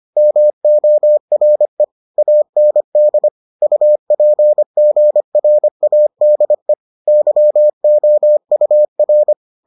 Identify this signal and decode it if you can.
Morse code, heard as a keyed tone, about 25 wpm, 600 Hz. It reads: MORE AND UPGRADE YOUR